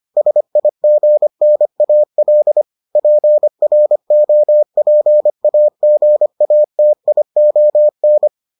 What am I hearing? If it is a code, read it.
Morse code, heard as a keyed tone, about 25 wpm, 600 Hz. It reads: SIGNAL PROPAGATION